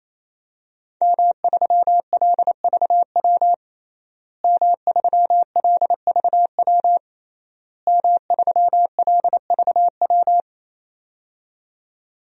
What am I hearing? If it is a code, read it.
Morse code, heard as a keyed tone, about 28 wpm, 700 Hz. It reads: M3LVW M3LVW M3LVW
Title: DXCC-100 Call Signs - 5 Characters - Repeated 3x - 28wpm